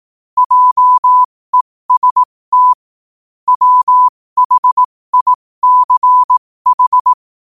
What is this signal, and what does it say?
Morse code, heard as a keyed tone, about 18 wpm, spaced as for 15 wpm, 1000 Hz. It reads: JEST WHICH
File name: qotdfiles / 15wpm